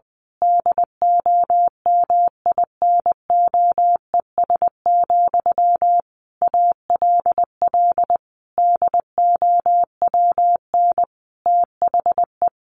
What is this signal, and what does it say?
Morse code, heard as a keyed tone, about 20 wpm, 700 Hz. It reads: DOMINOES, ALL DOWN THE
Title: A Sound Of Thunder 20wpm